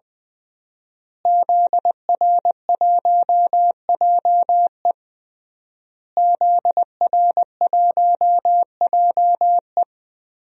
Morse code, heard as a keyed tone, about 21 wpm, 700 Hz. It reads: ZR1JE ZR1JE
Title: DXCC-100 Call Signs - 5 Characters - Repeated 3x - 20wpm